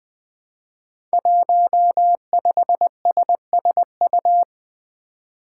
Morse code, heard as a keyed tone, about 20 wpm, 700 Hz. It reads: 15SSU
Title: Five Challenging Letter-Number Combinations 20wpm